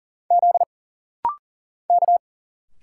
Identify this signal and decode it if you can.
Morse code, heard as a keyed tone, about 40 wpm, 700 Hz. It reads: Z K
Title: Single Letters - Rapid-Fire 40wpm